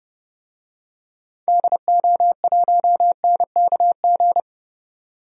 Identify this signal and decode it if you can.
Morse code, heard as a keyed tone, about 30 wpm, 700 Hz. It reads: DO1NKG
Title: DXCC-100 Call Signs - 6 Characters - 30wpm